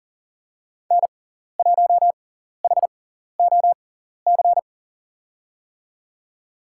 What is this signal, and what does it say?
Morse code, heard as a keyed tone, about 40 wpm, spaced as for 14 wpm, 700 Hz. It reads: N1HOC
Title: USA Call Signs - 6x Character Spacing - 40wpm